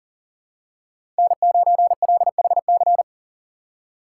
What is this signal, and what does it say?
Morse code, heard as a keyed tone, about 37 wpm, 700 Hz. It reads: N9LHC
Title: USA Call Signs - 5 Characters - 6x Character Spacing - 40wpm